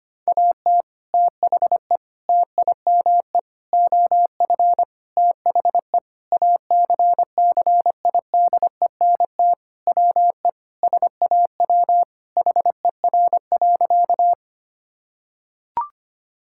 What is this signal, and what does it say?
Morse code, heard as a keyed tone, about 25 wpm, 700 Hz. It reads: AT THE TIME OF THE ACCIDENT WE SAW HER.
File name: practice / 25wpm / Sentences with 4-Word Phrases - 25wpm